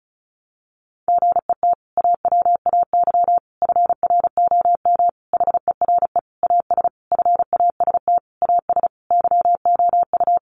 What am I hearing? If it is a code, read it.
Morse code, heard as a keyed tone, about 34 wpm, 700 Hz. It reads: GET AWAY FROM HERE AS FAST AS YOU
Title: Sentences from Top 600 Words 35wpm